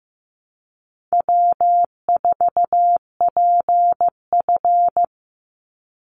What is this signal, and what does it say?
Morse code, heard as a keyed tone, about 15 wpm, 700 Hz. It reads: W4PF
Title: Easy Call Signs - 15wpm